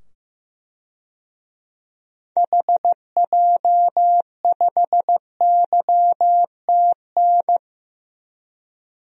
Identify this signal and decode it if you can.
Morse code, heard as a keyed tone, about 15 wpm, 700 Hz. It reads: HJ5YTN
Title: DXCC-100 Call Signs - 6 Characters - 15wpm